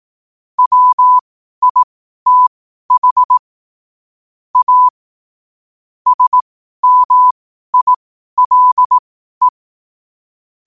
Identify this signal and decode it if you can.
Morse code, heard as a keyed tone, about 18 wpm, spaced as for 12 wpm, 1000 Hz. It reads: WITH A SMILE
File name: qotdfiles / 12wpm